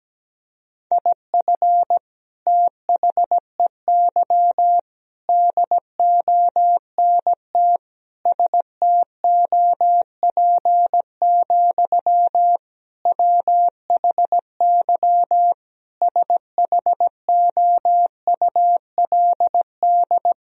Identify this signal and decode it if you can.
Morse code, heard as a keyed tone, about 17 wpm, 700 Hz. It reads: IF THEY DONT STOP, WHY SHOULD
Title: Sentences from Top 1000 Words - Review Part 1 - 17wpm